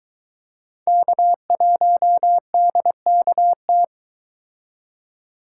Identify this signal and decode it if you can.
Morse code, heard as a keyed tone, about 23 wpm, 700 Hz. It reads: K1DKT